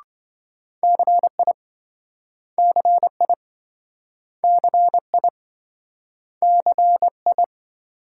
Morse code, heard as a keyed tone, about 27 wpm, 700 Hz. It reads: CI CI CI CI